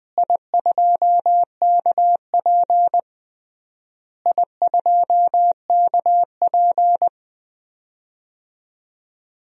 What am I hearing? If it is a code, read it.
Morse code, heard as a keyed tone, about 20 wpm, 700 Hz. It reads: I2KP I2KP
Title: DXCC-100 Call Signs - 4 Characters - Repeated 3x - 20wpm